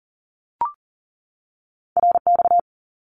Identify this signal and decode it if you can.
Morse code, heard as a keyed tone, about 42 wpm, 700 Hz. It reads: RX
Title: Abbreviations 40wpm